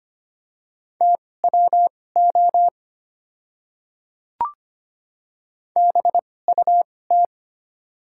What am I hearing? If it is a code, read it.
Morse code, heard as a keyed tone, about 25 wpm, spaced as for 18 wpm, 690 Hz. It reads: TWO BUT
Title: 2x Character Spacing - Top 100 Words - 25wpm